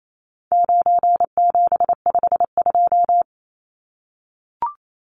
Morse code, heard as a keyed tone, about 28 wpm, 700 Hz. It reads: 9752
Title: Numbers - 4-Digit 28wpm